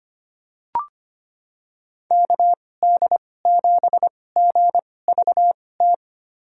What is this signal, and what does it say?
Morse code, heard as a keyed tone, about 25 wpm, spaced as for 18 wpm, 700 Hz. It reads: KD7GVT
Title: USA Call Signs - 6 Characters - 2x Character Spacing - 25wpm